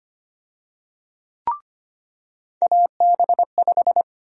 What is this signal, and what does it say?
Morse code, heard as a keyed tone, about 25 wpm, 700 Hz. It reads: AB5
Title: USA Call Sign Prefixes - 25wpm